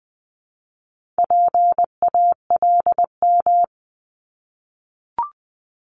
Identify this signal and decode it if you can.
Morse code, heard as a keyed tone, about 20 wpm, 700 Hz. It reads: PALM
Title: Multi-letter Phonemes Repeated 3x 20wpm